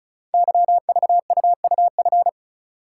Morse code, heard as a keyed tone, about 36 wpm, 700 Hz. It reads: YVUUF
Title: Code Groups - 5 Letter:Numbers - 35wpm